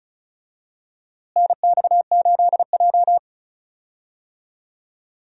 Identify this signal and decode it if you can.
Morse code, heard as a keyed tone, about 36 wpm, 700 Hz. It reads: NX8J
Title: USA Call Signs - 4 Characters - Repeated 3x - 35wpm